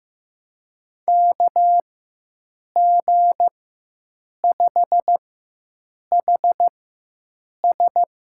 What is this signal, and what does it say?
Morse code, heard as a keyed tone, about 15 wpm, spaced as for 7 wpm, 700 Hz. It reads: KG5HS
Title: USA Call Signs - 4x Character Spacing - 15wpm